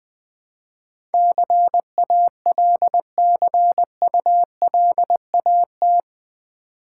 Morse code, heard as a keyed tone, about 20 wpm, 700 Hz. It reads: CALCULAT
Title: Build A Word - 4 Letters At A Time - 11-Letter Words - 20wpm